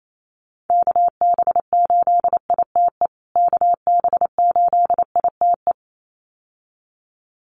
Morse code, heard as a keyed tone, about 28 wpm, 700 Hz. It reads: KB8ITE KB8ITE
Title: USA Call Signs Repeated 2x 28wpm